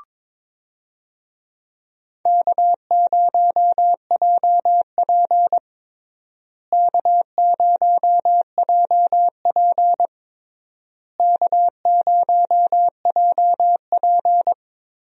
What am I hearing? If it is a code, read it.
Morse code, heard as a keyed tone, about 22 wpm, 700 Hz. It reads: K0JP K0JP K0JP
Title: USA Call Signs - 4 Characters - Repeated 3x - 22wpm